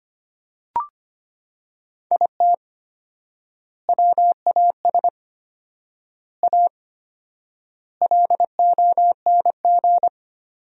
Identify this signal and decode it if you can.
Morse code, heard as a keyed tone, about 25 wpm, 700 Hz. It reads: IT WAS A LONG
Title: 4x Word-Spacing - Sentences from Top 500 Words - 25wpm